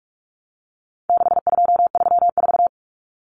Morse code, heard as a keyed tone, about 44 wpm, 700 Hz. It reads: 6234
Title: Common Cut Numbers - 4-Digit 45wpm